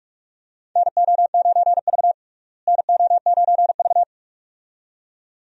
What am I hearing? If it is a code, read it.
Morse code, heard as a keyed tone, about 46 wpm, 700 Hz. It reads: NO9V NO9V
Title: USA Call Signs - 4 Characters - Repeated 2x - 45wpm